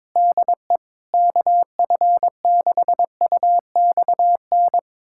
Morse code, heard as a keyed tone, about 22 wpm, 700 Hz. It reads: DE KF6UXN
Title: USA Callsign DE Callsign - 6 Characters - 22wpm